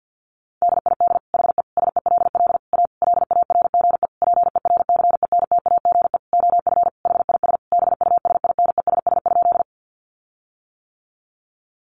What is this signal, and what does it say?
Morse code, heard as a keyed tone, about 48 wpm, 700 Hz. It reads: DID HE SELL A LARGE PERCENTAGE OF HIS BUSINESS?
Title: Sentences from Top 1300 Words 50wpm